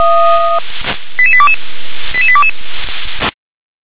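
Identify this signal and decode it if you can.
Five-tone selective call (ZVEI), heard as quick runs of five tones, tones of about 70 ms. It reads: 88822 88822
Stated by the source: ZVEI five-tone selective call for address 88835, received by SDR.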